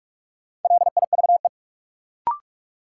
Morse code, heard as a keyed tone, about 42 wpm, 700 Hz. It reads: LIVE
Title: Sentences from Top 200 Words - New Word Drill - 45wpm